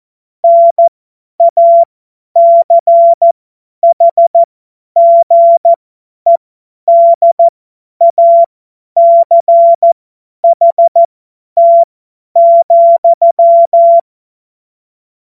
Morse code, heard as a keyed tone, about 14 wpm, spaced as for 10 wpm, 680 Hz. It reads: NACHGEDACHT,
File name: Morsekurs / Witze / Witz 10.Oktober 2023 50 F70